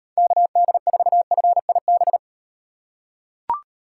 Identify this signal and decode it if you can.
Morse code, heard as a keyed tone, about 38 wpm, 700 Hz. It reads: KD4FIB